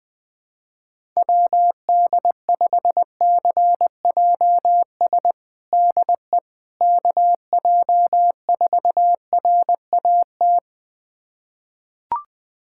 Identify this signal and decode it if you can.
Morse code, heard as a keyed tone, about 21 wpm, 700 Hz. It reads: WD5CJS DE KJ4RAT
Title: USA Callsign DE Callsign 20wpm